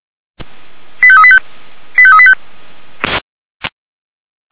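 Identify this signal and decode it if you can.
Five-tone selective call (ZVEI), heard as quick runs of five tones, tones of about 70 ms. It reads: 85385 85385
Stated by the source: Received by SDR, ZVEI five-tone selective call.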